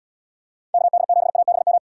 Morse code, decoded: KD7ABM

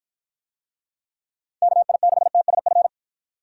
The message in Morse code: KI6TSF

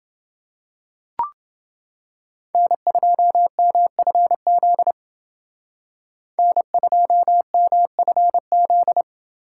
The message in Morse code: N2MFZ N2MFZ

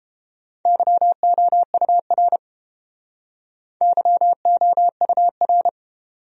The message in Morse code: YOUR YOUR

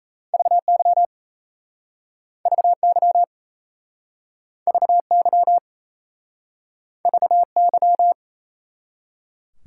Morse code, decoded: VY VY VY VY

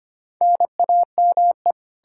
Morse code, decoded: NAME

